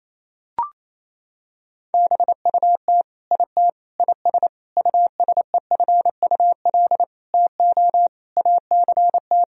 Morse code, heard as a keyed tone, about 28 wpm, 700 Hz. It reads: BUT IT IS USEFUL TO ACT
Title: Sentences from Top 1000 Words - Review Part 4 - 28wpm